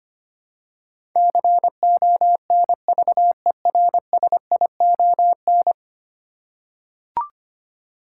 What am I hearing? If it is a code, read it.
Morse code, heard as a keyed tone, about 25 wpm, 700 Hz. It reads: CONVERSION
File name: Ten Letter Words - 25wpm